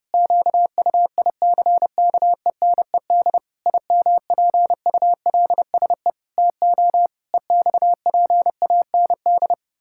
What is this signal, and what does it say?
Morse code, heard as a keyed tone, about 30 wpm, 700 Hz. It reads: QUICKENED IMPULSE TO EXPAND
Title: Pangram Challenge v2 - 30wpm